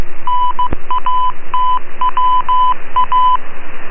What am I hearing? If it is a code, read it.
Morse code, heard as a keyed tone, about 15 wpm, 1000 Hz. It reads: NATWA